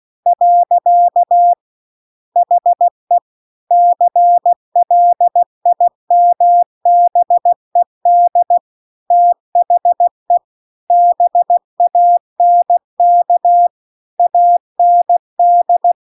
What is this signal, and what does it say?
Morse code, heard as a keyed tone, about 16 wpm, 700 Hz. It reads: . HE CLIMBED THE BANK AND